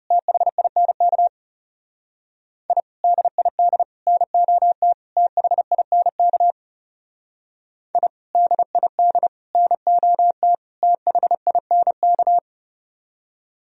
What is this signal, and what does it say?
Morse code, heard as a keyed tone, about 37 wpm, 700 Hz. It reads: THINK I DID NOT THINK I DID NOT THINK
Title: Speed-Racing - Sets of 4 Words 30wpm